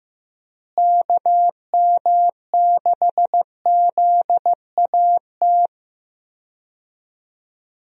Morse code, heard as a keyed tone, about 15 wpm, 700 Hz. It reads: KM6ZAT
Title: USA Call Signs - 6 Characters - Repeated 2x - 15wpm